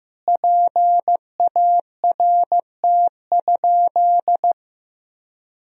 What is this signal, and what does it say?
Morse code, heard as a keyed tone, about 15 wpm, 700 Hz. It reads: PART?
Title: Sentences from Top 1000 Words - Review Part 4 - 15wpm